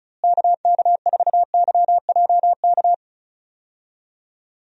Morse code, heard as a keyed tone, about 34 wpm, 700 Hz. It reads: KK4YJK
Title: USA Call Signs - 6 Characters - Repeated 2x - 35wpm